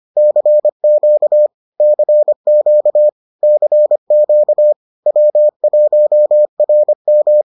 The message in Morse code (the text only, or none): CQ CQ CQ W1RM